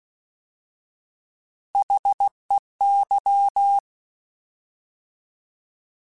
Morse code, heard as a keyed tone, about 16 wpm, 800 Hz. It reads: HEY
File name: Alphabetical / Alphabetical mp3s / Hs @15